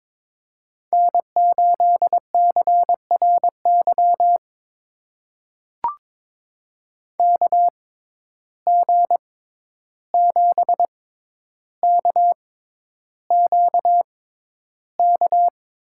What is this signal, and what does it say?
Morse code, heard as a keyed tone, about 22 wpm, spaced as for 11 wpm, 700 Hz. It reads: N8CRY KG7KQK
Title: USA Call Signs - 6x Character Spacing - 22wpm